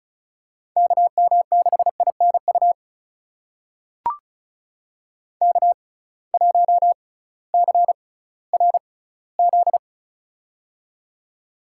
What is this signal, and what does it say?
Morse code, heard as a keyed tone, about 36 wpm, spaced as for 19 wpm, 700 Hz. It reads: KM6INU K1CRZ
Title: USA Call Signs - 6x Character Spacing - 35wpm